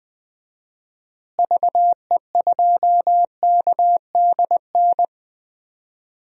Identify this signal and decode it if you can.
Morse code, heard as a keyed tone, about 20 wpm, 700 Hz. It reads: VE2KDN
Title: DXCC-100 Call Signs - 6 Characters - Repeated 3x - 20wpm